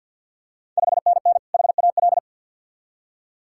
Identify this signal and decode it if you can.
Morse code, heard as a keyed tone, about 50 wpm, 690 Hz. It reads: 5NN HAL
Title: Ontario QSO Party Exchange 50wpm